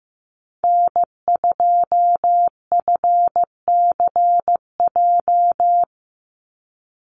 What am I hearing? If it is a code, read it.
Morse code, heard as a keyed tone, about 15 wpm, 700 Hz. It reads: N2FCJ
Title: USA Call Signs 15wpm